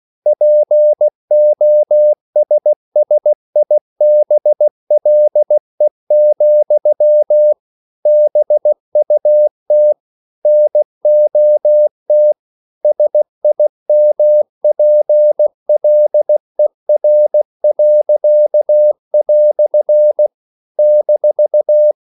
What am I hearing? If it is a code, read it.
Morse code, heard as a keyed tone, about 16 wpm, 600 Hz. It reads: POSSIBLE, BUT NOT SIMPLER." -
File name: CW / Practice / EnglishQuotations0216